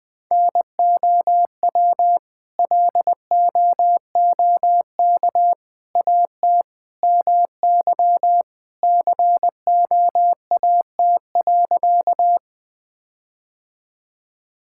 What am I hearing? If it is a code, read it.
Morse code, heard as a keyed tone, about 21 wpm, 700 Hz. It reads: NOW LOOK AT MY COAT.
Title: Sentences from Top 1600 Words 20wpm